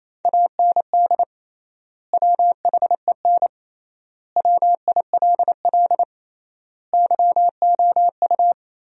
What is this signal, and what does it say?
Morse code, heard as a keyed tone, about 28 wpm, 700 Hz. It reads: AND WHEN WILL YOU
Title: 3x Word-Spacing - Sentences from Top 100 Words - 28wpm